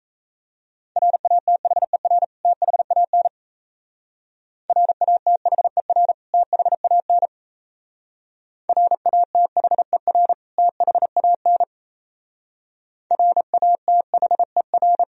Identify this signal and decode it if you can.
Morse code, heard as a keyed tone, about 34 wpm, 700 Hz. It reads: RATHER THAN RATHER THAN RATHER THAN RATHER
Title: Speed-Racing - Sets of 2 Words 28wpm